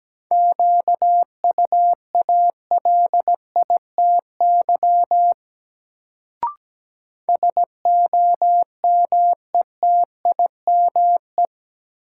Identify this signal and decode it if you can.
Morse code, heard as a keyed tone, about 17 wpm, 700 Hz. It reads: QUALITY SOMETIME
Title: Sentences from Top 1500 Words - New Word Drill Alt - 17wpm